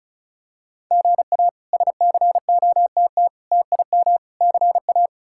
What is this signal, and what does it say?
Morse code, heard as a keyed tone, about 34 wpm, 700 Hz. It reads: GA SCOTT TIM CA
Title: K1USN SST QSO Practice - 35wpm